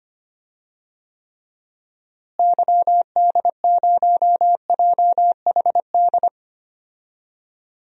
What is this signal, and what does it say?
Morse code, heard as a keyed tone, about 25 wpm, 700 Hz. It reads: YD0JHD
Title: DXCC-100 Call Signs - 6 Characters - Repeated 2x - 25wpm